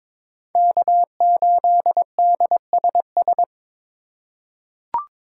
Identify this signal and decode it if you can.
Morse code, heard as a keyed tone, about 21 wpm, 700 Hz. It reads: K8DSS
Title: K1USN SST QSO Practice - 22wpm